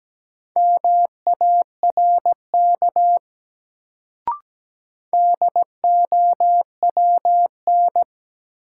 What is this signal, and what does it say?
Morse code, heard as a keyed tone, about 17 wpm, 700 Hz. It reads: MARK DOWN
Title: Sentences from Top 300 Words - New Word Drill - 17wpm